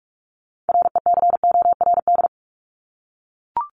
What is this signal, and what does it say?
Morse code, heard as a keyed tone, about 44 wpm, 700 Hz. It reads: RECORD
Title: Top 500 Words 45wpm